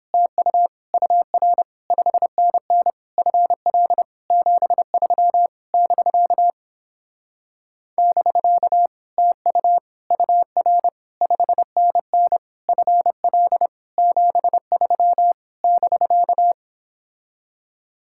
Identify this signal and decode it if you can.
Morse code, heard as a keyed tone, about 29 wpm, 700 Hz. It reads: TU UR 5NN FL 73 <BK> <BK> TU UR 5NN FL 73 <BK>